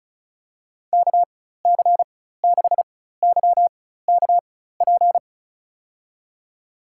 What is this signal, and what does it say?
Morse code, heard as a keyed tone, about 36 wpm, spaced as for 16 wpm, 700 Hz. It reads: KC6YKP
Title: USA Call Signs - 6 Characters - 4x Character Spacing - 35wpm